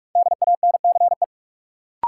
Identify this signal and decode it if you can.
Morse code, heard as a keyed tone, about 41 wpm, 700 Hz. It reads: DANCE
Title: Sentences from Top 700 Words - New Word Drill - 45wpm